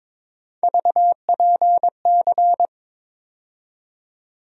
Morse code, heard as a keyed tone, about 22 wpm, 700 Hz. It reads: VPC
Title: USA Call Sign Postfixes - 3 Letters 22wpm